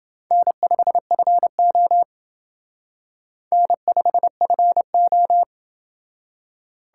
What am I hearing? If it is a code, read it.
Morse code, heard as a keyed tone, about 28 wpm, 700 Hz. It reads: N5FO N5FO